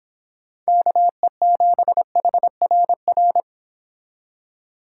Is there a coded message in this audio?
Yes: Morse code from 0.7 to 3.4 s, heard as a keyed tone, about 25 wpm, 700 Hz. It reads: KE7HRR